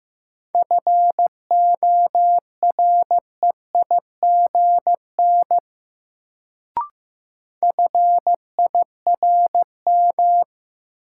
Morse code, heard as a keyed tone, about 15 wpm, 700 Hz. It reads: FOREIGN FIRM
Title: Sentences from Top 1400 Words - New Word Drill Alt - 15wpm